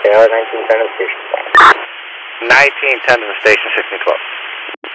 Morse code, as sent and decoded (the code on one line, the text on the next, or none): none